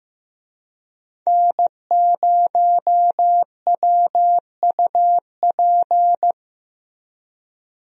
Morse code, heard as a keyed tone, about 15 wpm, 700 Hz. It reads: N0WUP
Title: USA Call Signs - 5 Characters - Repeated 2x - 15wpm